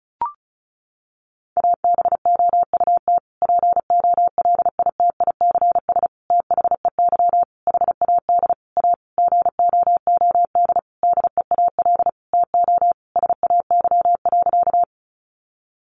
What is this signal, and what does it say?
Morse code, heard as a keyed tone, about 34 wpm, 700 Hz. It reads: ABOUT POLITICS THEY HAD A GOOD DEAL TO SAY.
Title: Sentences from Top 1000 Words 35wpm